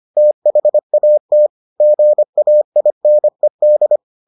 Morse code, heard as a keyed tone, about 25 wpm, 600 Hz. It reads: THAT GAINED